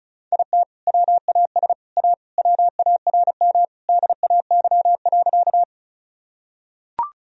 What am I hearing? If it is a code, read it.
Morse code, heard as a keyed tone, about 36 wpm, 700 Hz. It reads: IT WAS A WARM DAY.
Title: Sentences from Top 1000 Words - Review Part 2 - 35wpm